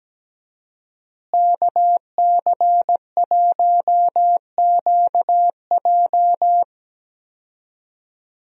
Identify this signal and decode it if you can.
Morse code, heard as a keyed tone, about 17 wpm, 700 Hz. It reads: KC1QJ
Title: USA Call Signs - 5 Characters - Repeated 2x - 17wpm